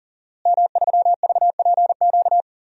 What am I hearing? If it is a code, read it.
Morse code, heard as a keyed tone, about 40 wpm, 700 Hz. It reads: M3VPQ
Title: DXCC-100 Call Signs - 5 Characters - Repeated 3x - 40wpm